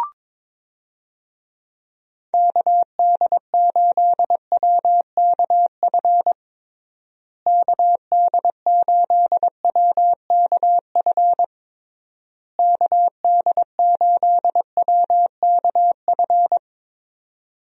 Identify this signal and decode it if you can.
Morse code, heard as a keyed tone, about 22 wpm, 700 Hz. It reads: KD8WKF KD8WKF KD8WKF
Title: USA Call Signs - 6 Characters - Repeated 3x - 22wpm